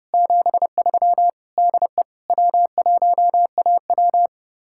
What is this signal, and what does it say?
Morse code, heard as a keyed tone, about 37 wpm, 700 Hz. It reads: 73 DE W1AW
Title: Speed-Racing Common QSO Phrases Encore - 30wpm